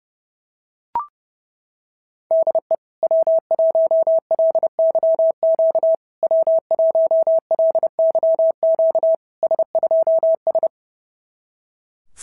Morse code, heard as a keyed tone, about 28 wpm, 690 Hz. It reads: DE W1LYQ W1LYQ S2S
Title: SOTA QSO Practice 30wpm